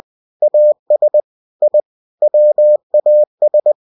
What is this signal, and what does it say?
Morse code, heard as a keyed tone, about 20 wpm, 600 Hz. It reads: AS I WAS
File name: cw-books / winnie-the-pooh / winnie-20wpm-page-0108